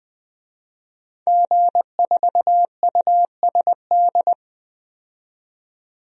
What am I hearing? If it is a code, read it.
Morse code, heard as a keyed tone, about 20 wpm, 700 Hz. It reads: G4USD
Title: DXCC-100 Call Signs - 5 Characters - 20wpm